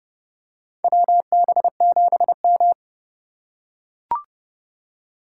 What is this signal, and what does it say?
Morse code, heard as a keyed tone, about 31 wpm, 700 Hz. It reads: WB7M
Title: USA Call Signs - 4 Characters - 8x Character Spacing - 30wpm